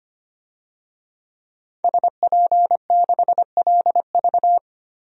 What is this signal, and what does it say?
Morse code, heard as a keyed tone, about 25 wpm, 700 Hz. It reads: SP6LV
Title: DXCC-100 Call Signs - 5 Characters - Repeated 2x - 25wpm